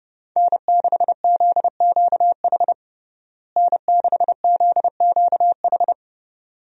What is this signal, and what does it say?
Morse code, heard as a keyed tone, about 30 wpm, 700 Hz. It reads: N6ZQH N6ZQH
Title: USA Call Signs - 5 Characters - Repeated 2x - 30wpm